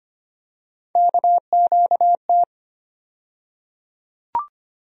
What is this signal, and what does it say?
Morse code, heard as a keyed tone, about 25 wpm, 700 Hz. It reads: KQT
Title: USA Call Sign Postfixes - 3 Letters 25wpm